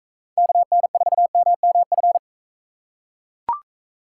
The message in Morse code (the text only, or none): KN4MMF